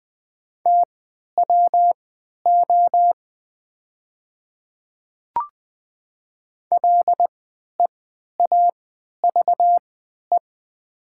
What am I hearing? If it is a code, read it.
Morse code, heard as a keyed tone, about 20 wpm, spaced as for 11 wpm, 700 Hz. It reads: TWO LEAVE
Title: 3x Character Spacing - Top 500 Words - 20wpm